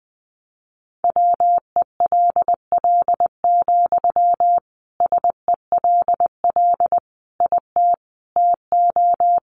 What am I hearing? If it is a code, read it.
Morse code, heard as a keyed tone, about 20 wpm, 700 Hz. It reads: WELL, SELL IT TO
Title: Sentences from Top 1100 Words 20wpm